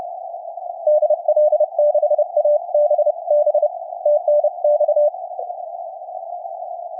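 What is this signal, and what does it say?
Morse code, heard as a keyed tone, about 31 wpm, 600 Hz. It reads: DL6ABB TNX
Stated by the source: CW on 14 MHz received on Lab599 Discovery TX-500, CW filter 100 Hz, recorded from line output.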